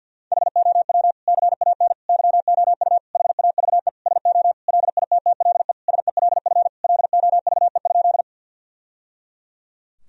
Morse code, heard as a keyed tone, about 49 wpm, 700 Hz. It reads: HOW CAN YOU HAVE SO LITTLE SELF LOVE?
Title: Sentences from Top 1000 Words - Review Part 1 - 50wpm